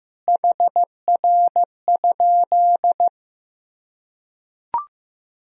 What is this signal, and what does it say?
Morse code, heard as a keyed tone, about 15 wpm, 700 Hz. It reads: HR?
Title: Sentences from Top 300 Words and Abbreviations 15wpm